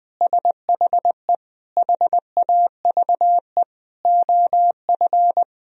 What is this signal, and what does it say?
Morse code, heard as a keyed tone, about 20 wpm, 700 Hz. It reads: SHE HAVE OF
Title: Sentences from Top 1000 Words - Review Part 3 - 20wpm